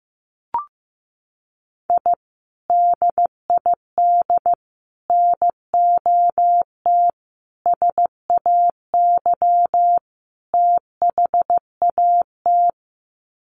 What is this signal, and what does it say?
Morse code, heard as a keyed tone, about 15 wpm, 700 Hz. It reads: I DID NOT SAY THAT
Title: Sets of 5 Words Encore2 15wpm